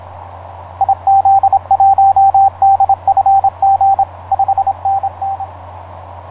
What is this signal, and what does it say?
Morse code, heard as a keyed tone, about 27 wpm, 780 Hz, with radio noise. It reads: IZ1DFG 5NN
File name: vk9x-g6ay ha7lw etc 10m cw 23feb2010 0836